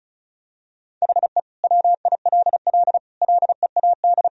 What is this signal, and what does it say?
Morse code, heard as a keyed tone, about 36 wpm, 700 Hz. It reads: HE WILL LEAD
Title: Sentences from Top 1000 Words - Review Part 2 - 35wpm